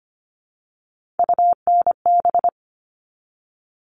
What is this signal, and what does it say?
Morse code, heard as a keyed tone, about 25 wpm, 700 Hz. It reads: UNB
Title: All Cut Numbers - 3-Digit 25wpm